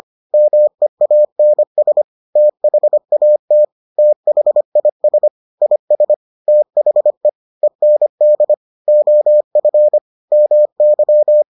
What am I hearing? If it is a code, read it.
Morse code, heard as a keyed tone, about 25 wpm, 600 Hz. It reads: MEANS THAT THIS IS THE END OF MY